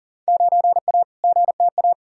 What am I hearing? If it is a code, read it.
Morse code, heard as a keyed tone, about 40 wpm, 700 Hz. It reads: 9A GTA
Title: Field Day Exchanges Encore - 40wpm